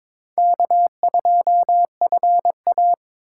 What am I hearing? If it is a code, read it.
Morse code, heard as a keyed tone, about 22 wpm, 700 Hz. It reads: K2FA
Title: USA Call Signs - 4 Characters - 6x Character Spacing - 22wpm